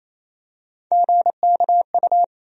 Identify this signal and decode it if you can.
Morse code, heard as a keyed tone, about 28 wpm, 700 Hz. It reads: GKU
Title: USA Call Sign Postfixes - 3 Letters 28wpm